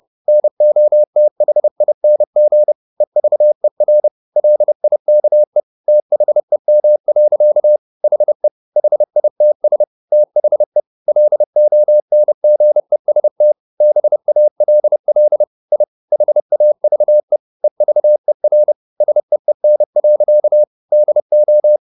Morse code, heard as a keyed tone, about 30 wpm, 600 Hz. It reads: NOTHING EVER LIKE THEM. HE HITS THE LONGEST BALL I HAVE EVER SEEN. DO
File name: the old man and the sea / the old man and the sea 1 30wpm